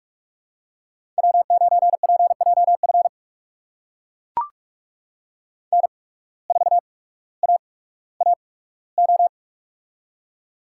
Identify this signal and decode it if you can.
Morse code, heard as a keyed tone, about 41 wpm, spaced as for 19 wpm, 700 Hz. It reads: W9PJF N4AAO
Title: USA Call Signs - 5 Characters - 8x Character Spacing - 45wpm